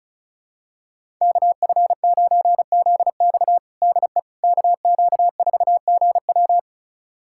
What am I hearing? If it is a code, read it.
Morse code, heard as a keyed tone, about 34 wpm, 700 Hz. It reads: KF9ZX DE KQ4GW
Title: USA Callsign DE Callsign - 5 Characters - 35wpm